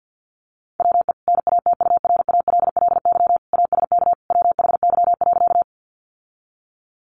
Morse code, heard as a keyed tone, about 50 wpm, 700 Hz. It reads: WE NATURALLY ASK WHY.
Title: Sentences from Top 1300 Words 50wpm